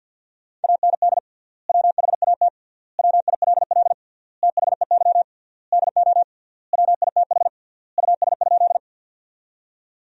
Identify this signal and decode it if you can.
Morse code, heard as a keyed tone, about 50 wpm, 700 Hz. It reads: AND WHAT WILL THEY DO WITH US?
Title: 3x Word-Spacing - Sentences from Top 200 Words - 50wpm